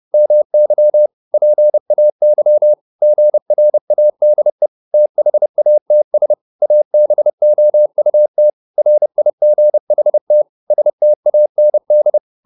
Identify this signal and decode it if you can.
Morse code, heard as a keyed tone, about 30 wpm, 600 Hz. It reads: MY PAY GRADE THATS ABOUT RIGHT STAND